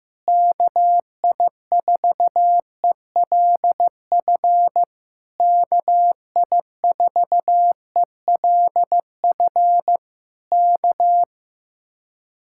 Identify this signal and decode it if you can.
Morse code, heard as a keyed tone, about 15 wpm, 700 Hz. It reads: KI4ELF KI4ELF K
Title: Speed-Racing Common QSO Phrases Encore - 15wpm